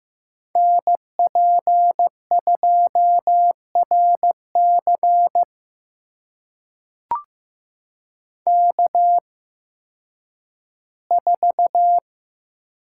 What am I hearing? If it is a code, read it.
Morse code, heard as a keyed tone, about 15 wpm, spaced as for 10 wpm, 700 Hz. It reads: NP2RC K4
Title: USA Call Signs - 5 Characters - 8x Character Spacing - 15wpm